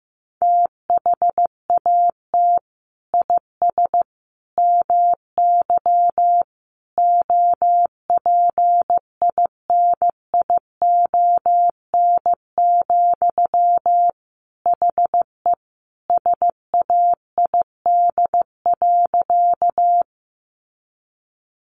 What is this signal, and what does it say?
Morse code, heard as a keyed tone, about 15 wpm, 700 Hz. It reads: THAT IS MY OPINION, HE SAID.
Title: Sentences from Top 1200 Words 15wpm